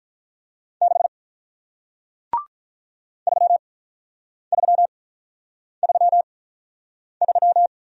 Morse code, heard as a keyed tone, about 47 wpm, spaced as for 9 wpm, 700 Hz. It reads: 6 3333